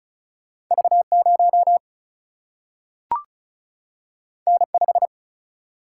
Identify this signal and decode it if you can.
Morse code, heard as a keyed tone, about 36 wpm, 700 Hz. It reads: V 0 N 5
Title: Single-Letter Single-Number 35wpm